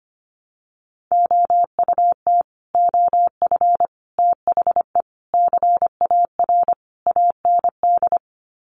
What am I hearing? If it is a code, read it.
Morse code, heard as a keyed tone, about 25 wpm, 700 Hz. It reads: OUT OF THE CAR AND
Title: Sets of 5 Words Encore 25wpm